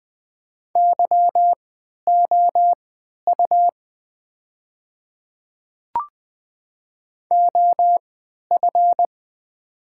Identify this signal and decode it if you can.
Morse code, heard as a keyed tone, about 20 wpm, spaced as for 11 wpm, 700 Hz. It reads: YOU OF
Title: 3x Character Spacing - Top 100 Words - 20wpm